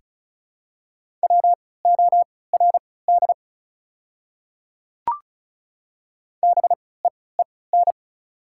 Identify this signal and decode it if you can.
Morse code, heard as a keyed tone, about 36 wpm, spaced as for 20 wpm, 700 Hz. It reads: WORD BEEN
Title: 3x Character Spacing - Top 200 Words - 35wpm